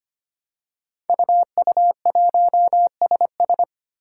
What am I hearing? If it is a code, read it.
Morse code, heard as a keyed tone, about 25 wpm, 700 Hz. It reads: UU1SS
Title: Five Challenging Letter-Number Combinations 25wpm